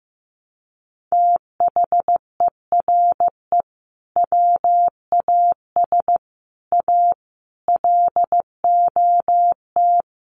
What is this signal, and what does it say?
Morse code, heard as a keyed tone, about 15 wpm, 700 Hz. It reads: THERE WAS A LOT
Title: Sets of 5 Words 15wpm